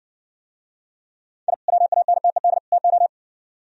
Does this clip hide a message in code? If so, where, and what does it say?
Yes: Morse code from 1.5 to 3.1 s, heard as a keyed tone, about 60 wpm, 700 Hz. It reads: I WANTED TO